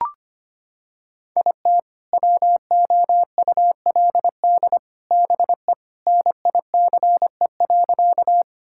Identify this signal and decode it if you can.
Morse code, heard as a keyed tone, about 25 wpm, 700 Hz. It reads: IT WOULD BE NICE.
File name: practice / 25wpm / Sentences with 3-Word Phrases - 25wpm